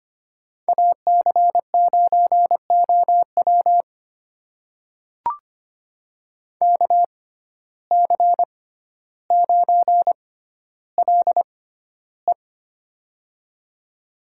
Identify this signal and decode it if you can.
Morse code, heard as a keyed tone, about 25 wpm, spaced as for 13 wpm, 700 Hz. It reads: AC9OW KC9LE
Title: USA Call Signs - 5 Characters - 6x Character Spacing - 25wpm